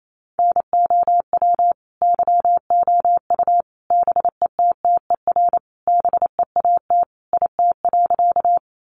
Morse code, heard as a keyed tone, about 28 wpm, 700 Hz. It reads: NOW YOU BETTER BEAT IT.
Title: Sentences from Top 1300 Words 28wpm